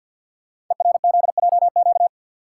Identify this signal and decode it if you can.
Morse code, heard as a keyed tone, about 50 wpm, 690 Hz. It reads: ERZJQ
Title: Code Groups - 5 Letters - 50wpm